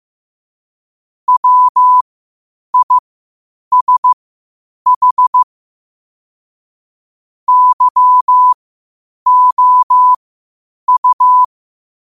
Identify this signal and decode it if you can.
Morse code, heard as a keyed tone, about 15 wpm, spaced as for 8 wpm, 1000 Hz. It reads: WISH YOU
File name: qotdfiles / 8wpm